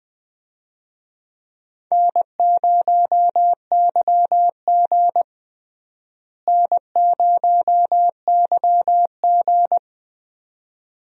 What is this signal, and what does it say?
Morse code, heard as a keyed tone, about 20 wpm, 700 Hz. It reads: N0YG N0YG
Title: USA Call Signs - 4 Characters - Repeated 2x - 20wpm